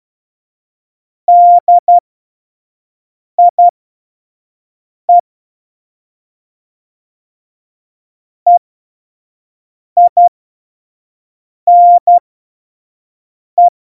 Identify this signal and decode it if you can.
Morse code, heard as a keyed tone, about 12 wpm, spaced as for 5 wpm, 700 Hz. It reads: DIE EINE
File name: Morsekurs / Witze / Witz 03 April 2018, 25 F70